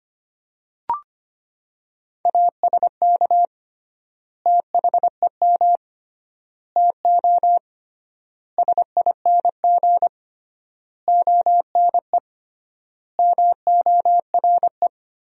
Morse code, heard as a keyed tone, about 25 wpm, 700 Hz. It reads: ASK THEM TO SING ONE MORE
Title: 3x Word-Spacing - Sentences from Top 400 Words - 25wpm